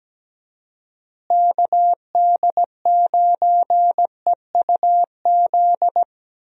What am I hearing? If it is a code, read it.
Morse code, heard as a keyed tone, about 17 wpm, 700 Hz. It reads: KD9EUZ
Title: USA Call Signs - 6 Characters - Repeated 3x - 17wpm